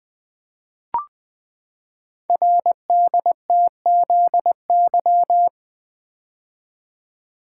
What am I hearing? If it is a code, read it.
Morse code, heard as a keyed tone, about 20 wpm, 700 Hz. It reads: RDTZY
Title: Code Groups - 5 Letters - 20wpm